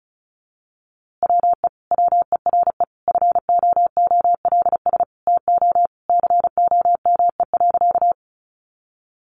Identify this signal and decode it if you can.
Morse code, heard as a keyed tone, about 34 wpm, 700 Hz. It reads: WE WERE FOOLS TO COME.
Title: A Sound Of Thunder 35wpm